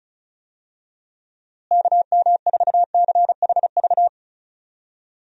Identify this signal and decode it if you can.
Morse code, heard as a keyed tone, about 36 wpm, 700 Hz. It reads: KM4CHV
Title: USA Call Signs - 6 Characters - Repeated 2x - 35wpm